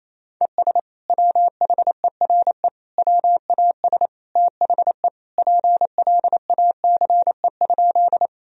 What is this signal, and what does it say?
Morse code, heard as a keyed tone, about 28 wpm, 700 Hz. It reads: ES WHERE WAS THE PLACE?
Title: Sentences from Top 200 Words and Abbreviations 28wpm